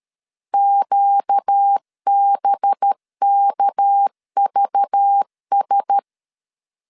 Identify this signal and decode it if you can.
Morse code, heard as a keyed tone, about 13 wpm, 800 Hz. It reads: QBKVS
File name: CW / DET / 1 9 1500C S B 12wpm 3819